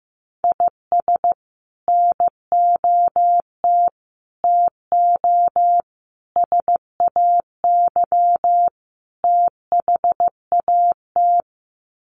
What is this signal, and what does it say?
Morse code, heard as a keyed tone, about 15 wpm, 700 Hz. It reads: IS NOT TO SAY THAT
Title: Sets of 5 Words 15wpm